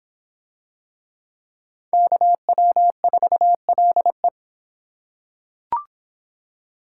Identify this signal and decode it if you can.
Morse code, heard as a keyed tone, about 23 wpm, 700 Hz. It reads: KW4LE